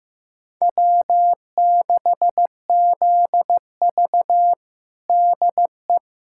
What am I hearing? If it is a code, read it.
Morse code, heard as a keyed tone, about 15 wpm, 700 Hz. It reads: W6ZV DE
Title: USA Callsign DE Callsign - 4 Characters - 15wpm